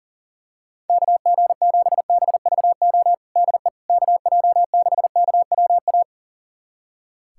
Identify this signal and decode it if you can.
Morse code, heard as a keyed tone, about 40 wpm, 700 Hz. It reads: KG7BVO DE KJ6KWA
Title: USA Callsign DE Callsign - 6 Characters - 40wpm